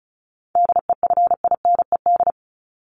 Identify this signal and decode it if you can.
Morse code, heard as a keyed tone, about 34 wpm, 700 Hz. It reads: DEFINED
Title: ED Suffix - Top 100 Words with Base Words 35wpm